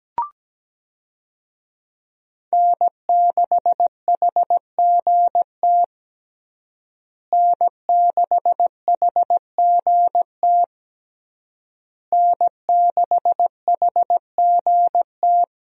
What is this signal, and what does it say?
Morse code, heard as a keyed tone, about 17 wpm, 700 Hz. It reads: N6HGT N6HGT N6HGT
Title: USA Call Signs - 5 Characters - Repeated 3x - 17wpm